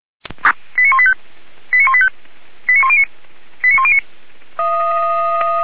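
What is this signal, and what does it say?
Five-tone selective call (ZVEI), heard as quick runs of five tones, tones of about 70 ms. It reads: 89185 89185 89109 89109
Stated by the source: ZVEI five-tone call received by SDR, on the feuerwehr-orlamuende.org server (fire brigade Orlamünde).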